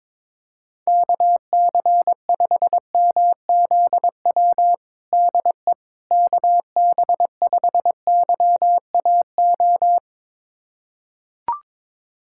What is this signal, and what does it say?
Morse code, heard as a keyed tone, about 22 wpm, 700 Hz. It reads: KC5MZW DE KB5YAO
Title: USA Callsign DE Callsign - 6 Characters - 22wpm